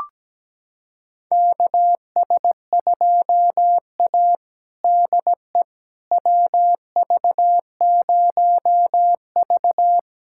Morse code, heard as a keyed tone, about 17 wpm, 700 Hz. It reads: KS2A DE WV0V
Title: USA Callsign DE Callsign - 4 Characters - 17wpm